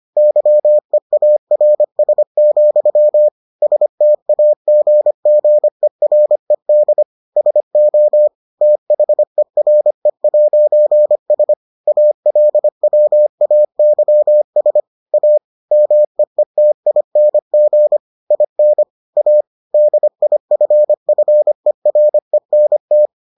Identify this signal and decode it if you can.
Morse code, heard as a keyed tone, about 25 wpm, 600 Hz. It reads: YEARS, STAGGERED SO THERE'S ALWAYS A MEETING IN A DIFFERENT